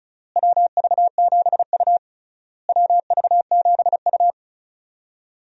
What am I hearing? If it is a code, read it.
Morse code, heard as a keyed tone, about 34 wpm, 700 Hz. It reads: WV7U WV7U
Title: USA Call Signs - 4 Characters - Repeated 3x - 35wpm